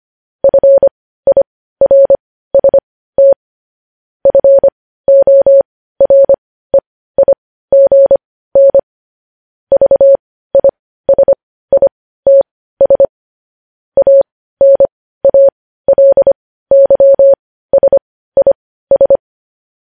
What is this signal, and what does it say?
Morse code, heard as a keyed tone, about 25 wpm, spaced as for 15 wpm, 550 Hz. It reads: FIRST FOREIGN VISITS ANALYSIS
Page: news headlines in morse code at 15 wpm